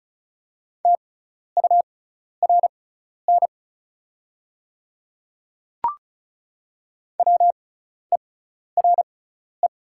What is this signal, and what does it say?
Morse code, heard as a keyed tone, about 36 wpm, spaced as for 12 wpm, 700 Hz. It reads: TURN WERE
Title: 6x Character Spacing - Top 200 Words - 35wpm